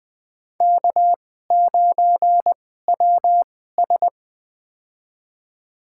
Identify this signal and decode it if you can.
Morse code, heard as a keyed tone, about 20 wpm, spaced as for 14 wpm, 700 Hz. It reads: K9WS
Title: USA Call Signs - 4 Characters - 2x Character Spacing - 20wpm